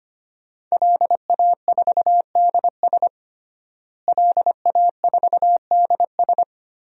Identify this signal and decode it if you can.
Morse code, heard as a keyed tone, about 25 wpm, 700 Hz. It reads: LA4DS LA4DS
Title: DXCC-100 Call Signs - 5 Characters - Repeated 3x - 25wpm